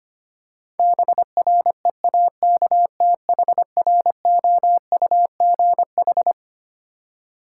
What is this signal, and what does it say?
Morse code, heard as a keyed tone, about 25 wpm, 700 Hz. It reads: BREAKTHROUGH
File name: Twelve Letter Words - 25wpm